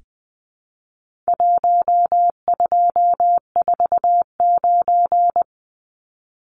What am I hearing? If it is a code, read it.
Morse code, heard as a keyed tone, about 20 wpm, 700 Hz. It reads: 1249
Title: Numbers - 4-Digit 20wpm